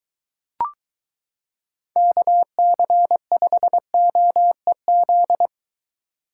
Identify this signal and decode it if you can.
Morse code, heard as a keyed tone, about 23 wpm, 700 Hz. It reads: KC5OEZ